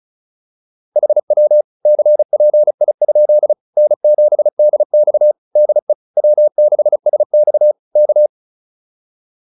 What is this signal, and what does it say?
Morse code, heard as a keyed tone, about 34 wpm, 600 Hz. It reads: HW CPI? N7DX DE W6SX K
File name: QSO102 35